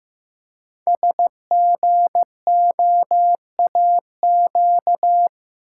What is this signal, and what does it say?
Morse code, heard as a keyed tone, about 15 wpm, 700 Hz. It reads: SGOAQ
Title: Code Groups - 5 Letters - 15wpm